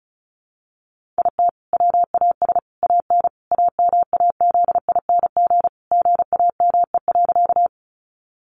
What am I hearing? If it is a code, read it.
Morse code, heard as a keyed tone, about 34 wpm, 700 Hz. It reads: IT WAS AN AMAZING GAME.
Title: Sentences from Top 1500 Words 35wpm